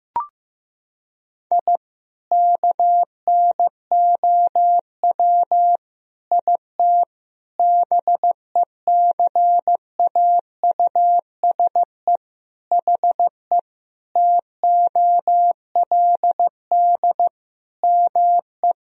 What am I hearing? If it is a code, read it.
Morse code, heard as a keyed tone, about 15 wpm, 700 Hz. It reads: I KNOW IT BECAUSE HE TOLD ME
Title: Sentences from Top 1000 Words - Review Part 3 - 15wpm